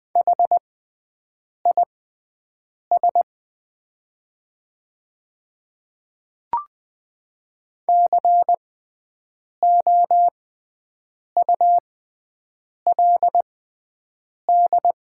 Morse code, heard as a keyed tone, about 20 wpm, spaced as for 7 wpm, 690 Hz. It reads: HIS COULD
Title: 6x Character Spacing - Top 100 Words - 20wpm